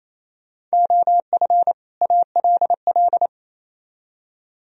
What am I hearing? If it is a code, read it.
Morse code, heard as a keyed tone, about 36 wpm, 700 Hz. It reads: OF ALL